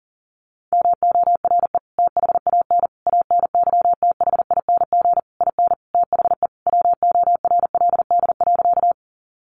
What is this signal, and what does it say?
Morse code, heard as a keyed tone, about 40 wpm, 700 Hz. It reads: MORE THAN ANYTHING IN THE WORLD.
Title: Sentences from Top 1000 Words 40wpm